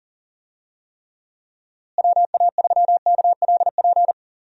Morse code, heard as a keyed tone, about 40 wpm, 700 Hz. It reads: WA3KLP
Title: USA Call Signs - 6 Characters - Repeated 2x - 40wpm